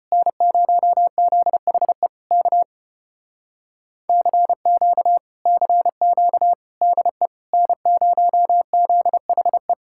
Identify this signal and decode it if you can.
Morse code, heard as a keyed tone, about 33 wpm, 700 Hz. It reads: N0ZHE K CQ CQ DE N0ZHE